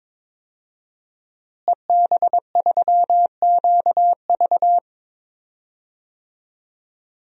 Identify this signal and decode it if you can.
Morse code, heard as a keyed tone, about 22 wpm, 700 Hz. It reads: EB3QV
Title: DXCC-100 Call Signs - 5 Characters - Repeated 3x - 22wpm